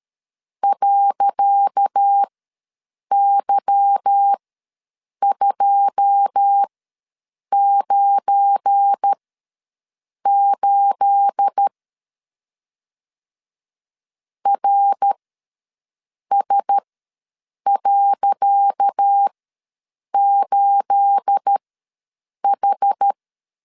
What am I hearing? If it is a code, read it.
Morse code, heard as a keyed tone, about 12 wpm, spaced as for 7 wpm, 800 Hz. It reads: .Y298 RS.8H
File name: LEZIONI / 10 SIGNS / 10C DOT ws 12wpm 3548